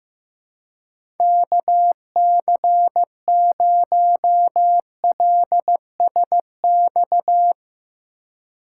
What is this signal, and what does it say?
Morse code, heard as a keyed tone, about 15 wpm, 700 Hz. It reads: KC0LSX
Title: USA Call Signs - 4x Character Spacing - 15wpm